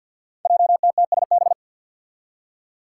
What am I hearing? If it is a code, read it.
Morse code, heard as a keyed tone, about 52 wpm, 700 Hz. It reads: JTTSB
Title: Five Challenging Letter-Number Combinations 50wpm